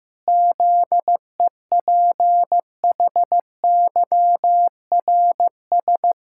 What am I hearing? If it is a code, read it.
Morse code, heard as a keyed tone, about 15 wpm, 700 Hz. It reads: ZEPHYRS
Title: Pangram Challenge v2 - 15wpm